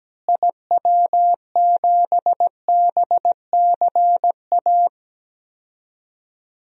Morse code, heard as a keyed tone, about 17 wpm, 700 Hz. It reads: IW7BCA
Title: DXCC-100 Call Signs - 6 Characters - Repeated 2x - 17wpm